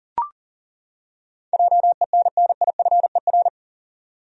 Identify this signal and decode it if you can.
Morse code, heard as a keyed tone, about 40 wpm, 700 Hz. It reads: JENNIFER
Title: USA Common Names - 40wpm